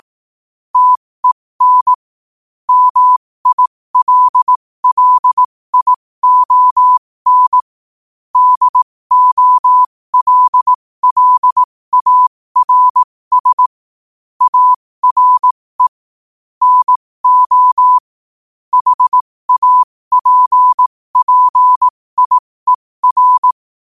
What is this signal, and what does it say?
Morse code, heard as a keyed tone, about 18 wpm, spaced as for 15 wpm, 1000 Hz. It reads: TEN MILLION DOLLARS ARE NO HAPPIER
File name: qotdfiles / 15wpm